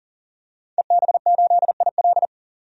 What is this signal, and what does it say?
Morse code, heard as a keyed tone, about 40 wpm, 700 Hz. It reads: EB8IL
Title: DXCC-100 Call Signs - 5 Characters - 40wpm